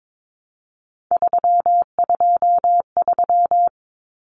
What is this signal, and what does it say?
Morse code, heard as a keyed tone, about 22 wpm, 700 Hz. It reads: 323
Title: Common Cut Numbers - 3-Digit 22wpm